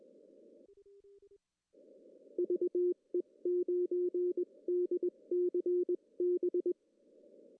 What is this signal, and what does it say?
Morse code, heard as a keyed tone, about 20 wpm, spaced as for 18 wpm, 360 Hz. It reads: VE9DCB
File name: CW / cqww-cw-k5tr